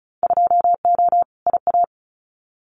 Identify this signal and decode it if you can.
Morse code, heard as a keyed tone, about 36 wpm, 690 Hz. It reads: 2O IA
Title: Winter Field Day Exchanges 35wpm